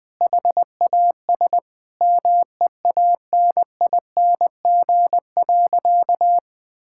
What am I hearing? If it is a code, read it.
Morse code, heard as a keyed tone, about 21 wpm, 700 Hz. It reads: HAS MEANING.
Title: Sentences with 2-Word Phrases - 20wpm